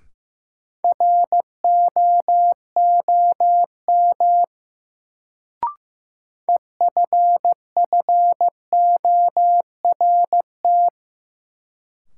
Morse code, heard as a keyed tone, about 15 wpm, 700 Hz. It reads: ROOM EFFORT
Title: Sets of 4 Words - New Word Drill - 15wpm